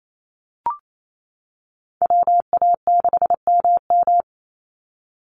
Morse code, heard as a keyed tone, about 28 wpm, 700 Hz. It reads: WA6MM
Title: NA SOTA Activator Callsigns 28wpm